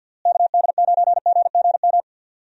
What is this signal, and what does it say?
Morse code, heard as a keyed tone, about 52 wpm, 700 Hz. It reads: KD9GGM
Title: USA Call Signs - 6 Characters - Repeated 3x - 50wpm